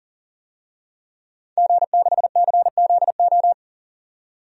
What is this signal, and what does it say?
Morse code, heard as a keyed tone, about 40 wpm, 700 Hz. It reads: G6CZO
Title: DXCC-100 Call Signs - 5 Characters - 40wpm